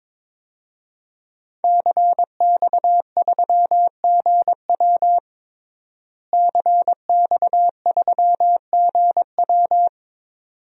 Morse code, heard as a keyed tone, about 21 wpm, 700 Hz. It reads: CX3GW CX3GW
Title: DXCC-100 Call Signs - 5 Characters - Repeated 2x - 22wpm